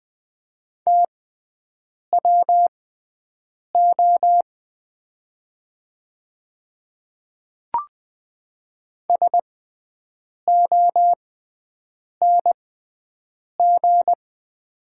Morse code, heard as a keyed tone, about 20 wpm, spaced as for 7 wpm, 700 Hz. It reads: TWO SONG
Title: 6x Character Spacing - Top 500 Words - 20wpm